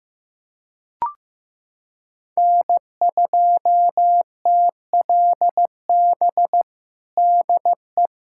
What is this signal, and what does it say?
Morse code, heard as a keyed tone, about 15 wpm, 700 Hz. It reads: N2TLB DE
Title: USA Callsign DE Callsign - 5 Characters - 15wpm